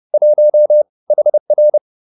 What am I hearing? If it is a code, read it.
Morse code, heard as a keyed tone, about 30 wpm, 600 Hz. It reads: 1 HR